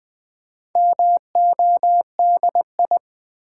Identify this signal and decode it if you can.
Morse code, heard as a keyed tone, about 20 wpm, 700 Hz. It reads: MODI
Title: Build A Word - 2 Letters At A Time - 6-Letter Words - 20wpm